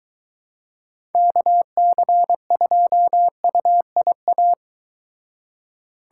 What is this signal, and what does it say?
Morse code, heard as a keyed tone, about 23 wpm, 700 Hz. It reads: KC2UIA